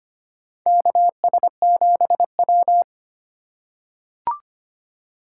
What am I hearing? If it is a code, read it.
Morse code, heard as a keyed tone, about 25 wpm, 700 Hz. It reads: KS7W
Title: USA Call Signs - 4x Character Spacing - 25wpm